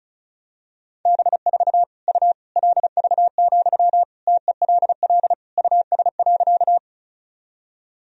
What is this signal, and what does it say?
Morse code, heard as a keyed tone, about 36 wpm, 700 Hz. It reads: B4 U LV, TELL US.
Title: Sentences from Top 400 Words and Abbreviations 35wpm